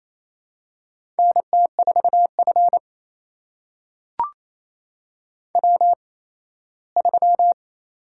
Morse code, heard as a keyed tone, about 27 wpm, spaced as for 17 wpm, 700 Hz. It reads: NT4F W3
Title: USA Call Signs - 4 Characters - 8x Character Spacing - 28wpm